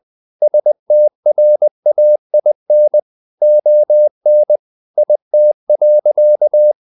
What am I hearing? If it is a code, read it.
Morse code, heard as a keyed tone, about 20 wpm, 600 Hz. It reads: STRAIN ON IT.
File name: the old man and the sea / the old man and the sea part 1 20wpm